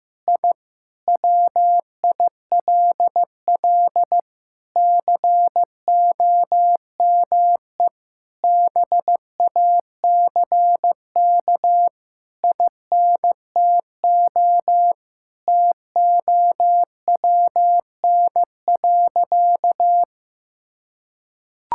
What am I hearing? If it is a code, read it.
Morse code, heard as a keyed tone, about 15 wpm, 700 Hz. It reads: I WILL COME BACK INTO TOWN.
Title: Sentences from Top 1000 Words - Review Part 6 - 15wpm